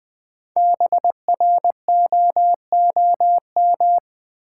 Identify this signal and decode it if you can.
Morse code, heard as a keyed tone, about 20 wpm, 700 Hz. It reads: BROOM
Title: Easy Words with Hints - 20wpm